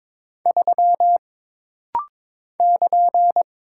Morse code, heard as a keyed tone, about 22 wpm, 700 Hz. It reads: 3 <KN>
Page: This rapid-fire timing is geared to push your ICR (Instant Character Recognition) skills.